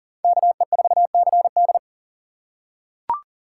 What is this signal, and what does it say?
Morse code, heard as a keyed tone, about 40 wpm, 700 Hz. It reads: KE4CD
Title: USA Call Signs - 5 Characters - 2x Character Spacing - 40wpm